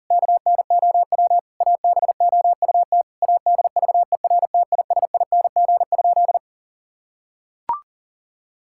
Morse code, heard as a keyed tone, about 40 wpm, 700 Hz. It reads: KNOW ABOUT ADVERTISING?
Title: Sentences from Top 1000 Words - Review Part 6 - 40wpm